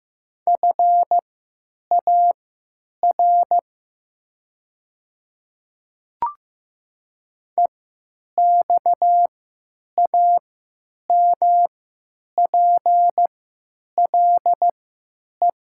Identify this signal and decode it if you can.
Morse code, heard as a keyed tone, about 15 wpm, spaced as for 9 wpm, 700 Hz. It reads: FAR EXAMPLE
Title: 3x Character Spacing - Top 500 Words - 15wpm